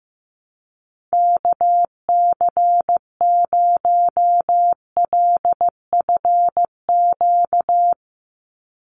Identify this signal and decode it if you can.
Morse code, heard as a keyed tone, about 15 wpm, 700 Hz. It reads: KC0LFQ
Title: USA Call Signs 15wpm